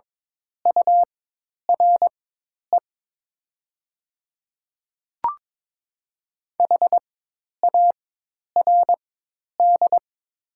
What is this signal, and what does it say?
Morse code, heard as a keyed tone, about 22 wpm, spaced as for 10 wpm, 700 Hz. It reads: URE HARD
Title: 4x Character Spacing - Top 300 Words - 22wpm